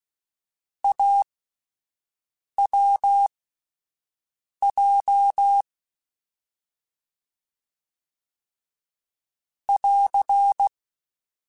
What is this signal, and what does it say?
Morse code, heard as a keyed tone, about 16 wpm, spaced as for 5 wpm, 800 Hz. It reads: AWJ <AR>
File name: Difficult Combos / Difficult Combos-mp3s / AWJ1 @5